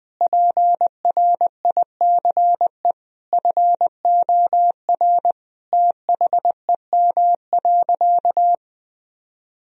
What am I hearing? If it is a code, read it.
Morse code, heard as a keyed tone, about 21 wpm, 700 Hz. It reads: PRICE FOR THEM.
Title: Sentences from Top 1000 Words - Review Part 4 - 20wpm